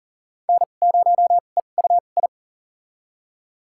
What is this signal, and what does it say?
Morse code, heard as a keyed tone, about 40 wpm, spaced as for 29 wpm, 700 Hz. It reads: N0EUI
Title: USA Call Signs - 5 Characters - 2x Character Spacing - 40wpm